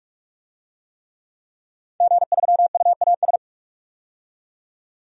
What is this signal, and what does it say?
Morse code, heard as a keyed tone, about 46 wpm, 700 Hz. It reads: G3UAS
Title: DXCC-100 Call Signs - 5 Characters - 45wpm